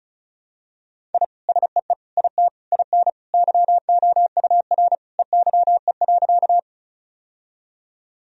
Morse code, heard as a keyed tone, about 36 wpm, 700 Hz. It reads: I SEE IT IN YOUR EYE.
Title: Sentences from Top 1000 Words - Review Part 1 - 35wpm